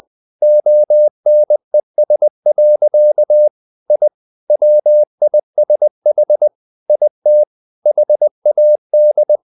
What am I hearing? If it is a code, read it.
Morse code, heard as a keyed tone, about 20 wpm, 600 Hz. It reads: ONES. I WISH IT HAD
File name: the old man and the sea part 2 20wpm